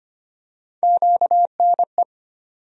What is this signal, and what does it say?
Morse code, heard as a keyed tone, about 25 wpm, 700 Hz. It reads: QNE
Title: USA Call Sign Postfixes - 3 Letters 25wpm